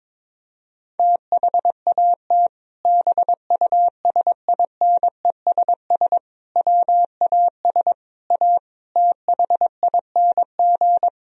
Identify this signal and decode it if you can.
Morse code, heard as a keyed tone, about 22 wpm, 700 Hz. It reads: THAT BUSINESS WAS A THING
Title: Sentences from Top 1000 Words - Review Part 3 - 22wpm